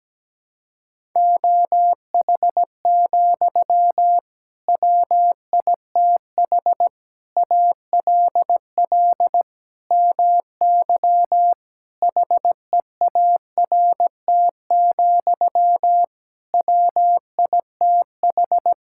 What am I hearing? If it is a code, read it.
Morse code, heard as a keyed tone, about 17 wpm, 700 Hz. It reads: OH, WITH ALL MY HEART, WITH
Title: Sentences from Top 1600 Words 17wpm